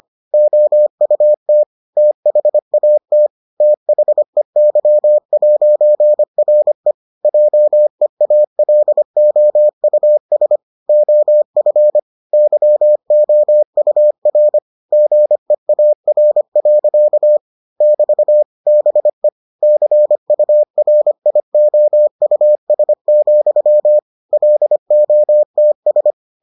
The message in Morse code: OUT THAT THEY'RE JEALOUS OF YOUR GEAR. <BT> BE CURIOUS, LOTS

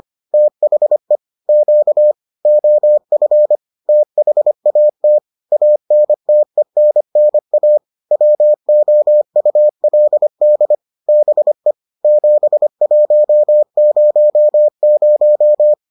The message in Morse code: THE Q OF THAT ANTENNA WOULD BE 7100

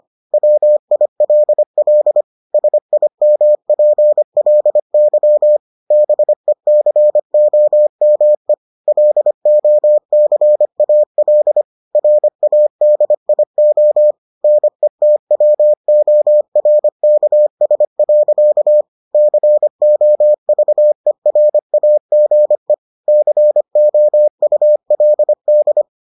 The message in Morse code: WILL SIMPLY BECOME LOCAL RADIO NETWORKS. COVERAGE COULD

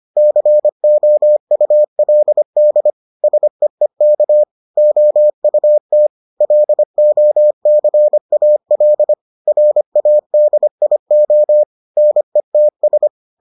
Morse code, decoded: COULD SEEK OUT LOCAL RADIO NETS

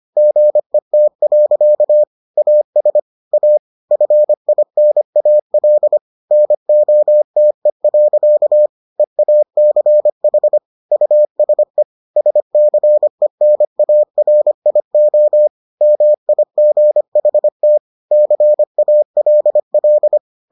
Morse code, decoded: GET. AS A FINAL NOTE. EACH USE SCENARIO MIGHT CALL

